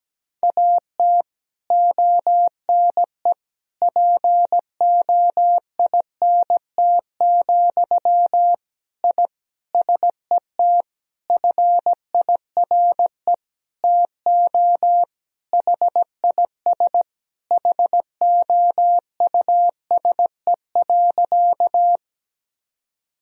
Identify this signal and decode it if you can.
Morse code, heard as a keyed tone, about 17 wpm, 700 Hz. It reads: AT ONE POINT, I SET FIRE TO HIS HOUSE.
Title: Sentences from Top 1000 Words - Review Part 1 - 17wpm